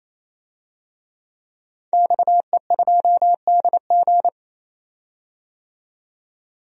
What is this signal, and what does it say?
Morse code, heard as a keyed tone, about 27 wpm, 700 Hz. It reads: XE2DG
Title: DXCC-100 Call Signs - 5 Characters - 28wpm